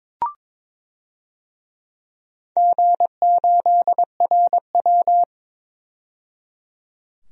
Morse code, heard as a keyed tone, about 22 wpm, 700 Hz. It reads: G8RW
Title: DXCC-100 Call Signs - 4 Characters - 22wpm